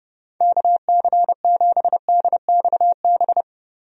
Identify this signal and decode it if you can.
Morse code, heard as a keyed tone, about 30 wpm, 700 Hz. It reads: KC7DXB
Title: USA Call Signs - 6 Characters - 30wpm